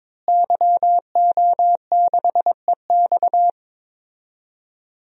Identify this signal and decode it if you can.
Morse code, heard as a keyed tone, about 22 wpm, 700 Hz. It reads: YO6EX
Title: DXCC-100 Call Signs - 5 Characters - Repeated 3x - 22wpm